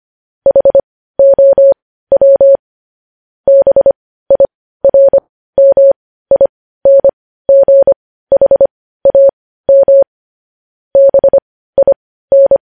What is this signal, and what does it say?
Morse code, heard as a keyed tone, about 25 wpm, spaced as for 15 wpm, 550 Hz. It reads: HOW BIRMINGHAM BIN
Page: news-headlines-in-morse-code-at-15-wpm